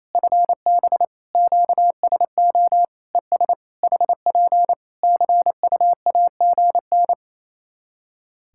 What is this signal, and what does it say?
Morse code, heard as a keyed tone, about 28 wpm, 700 Hz. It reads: FB QSO ES HP CUAGN
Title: K1USN SST QSO Practice 28wpm